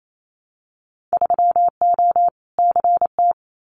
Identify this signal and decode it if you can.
Morse code, heard as a keyed tone, about 28 wpm, 700 Hz. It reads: 3O CT
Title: Winter Field Day Exchanges 28wpm